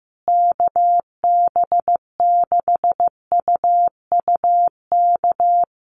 Morse code, heard as a keyed tone, about 15 wpm, 700 Hz. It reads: KB6UUK
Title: USA Call Signs Repeated 3x 15wpm